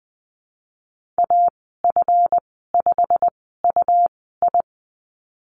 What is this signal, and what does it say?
Morse code, heard as a keyed tone, about 20 wpm, spaced as for 15 wpm, 700 Hz. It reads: AF5UI
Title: 161 Lesson 26g - Letter g - Call Signs 20wpm